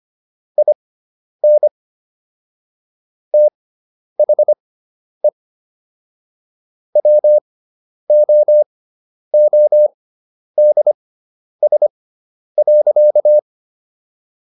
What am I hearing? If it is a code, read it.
Morse code, heard as a keyed tone, about 25 wpm, spaced as for 10 wpm, 600 Hz. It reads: IN THE WOODS.